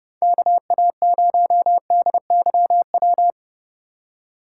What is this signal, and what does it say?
Morse code, heard as a keyed tone, about 36 wpm, 700 Hz. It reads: KA0DYW
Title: Speed Racing - USA Call Signs 30wpm